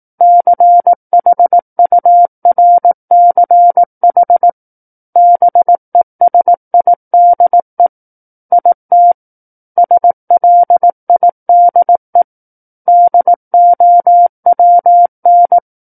Morse code, heard as a keyed tone, about 18 wpm, 700 Hz. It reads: CHURCH BESIDE IT SLIDE DOWN
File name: Book One / 09-Chapter 9 - 18 WPM